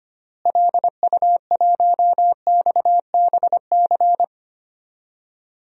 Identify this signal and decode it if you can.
Morse code, heard as a keyed tone, about 25 wpm, 700 Hz. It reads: LU1XBC
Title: DXCC-100 Call Signs - 6 Characters - Repeated 3x - 25wpm